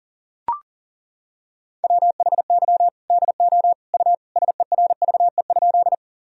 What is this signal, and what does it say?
Morse code, heard as a keyed tone, about 37 wpm, 700 Hz. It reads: WHY DO U SERVE?
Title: Sentences from Top 400 Words and Abbreviations 40wpm